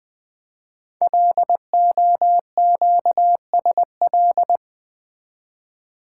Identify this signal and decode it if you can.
Morse code, heard as a keyed tone, about 20 wpm, 700 Hz. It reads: LOQSL
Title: Code Groups - 5 Letters - 20wpm